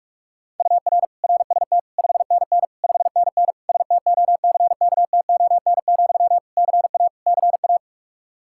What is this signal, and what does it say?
Morse code, heard as a keyed tone, about 46 wpm, 700 Hz. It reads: UR RST 5NN 5NN STOCKTON, CA CA
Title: Practice QSO - Exchange 1 - Beginning - 45wpm